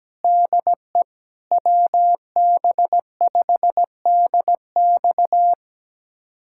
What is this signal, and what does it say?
Morse code, heard as a keyed tone, about 17 wpm, 700 Hz. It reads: DE WB5DX
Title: USA Callsign DE Callsign - 5 Characters - 17wpm